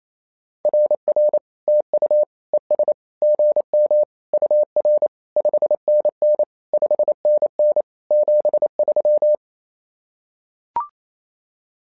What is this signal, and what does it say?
Morse code, heard as a keyed tone, about 27 wpm, 600 Hz. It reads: RR TU ES GM UR 5NN 5NN 73
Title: SOTA QSO Practice 28wpm